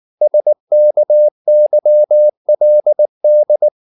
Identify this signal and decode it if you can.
Morse code, heard as a keyed tone, about 19 wpm, 600 Hz. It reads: SKYLD
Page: Ferie 19 wpm